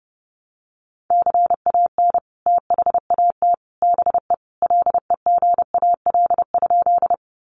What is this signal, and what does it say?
Morse code, heard as a keyed tone, about 30 wpm, 700 Hz. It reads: CAN THAT BE LEGAL?
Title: Sentences from Top 1200 Words 30wpm